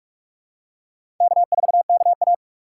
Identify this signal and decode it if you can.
Morse code, heard as a keyed tone, about 43 wpm, 700 Hz. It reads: K4KA
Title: USA Call Signs - 6x Character Spacing - 45wpm